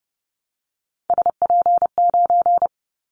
Morse code, heard as a keyed tone, about 30 wpm, 700 Hz. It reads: SP9
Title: Callsign Prefixes for Top 25 DXCC - Repeated 3x - 30wpm